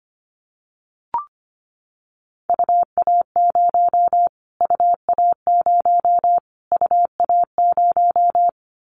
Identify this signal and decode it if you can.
Morse code, heard as a keyed tone, about 25 wpm, 700 Hz. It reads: UA0 UA0 UA0
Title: Callsign Prefixes for Top 25 DXCC - Repeated 3x - 25wpm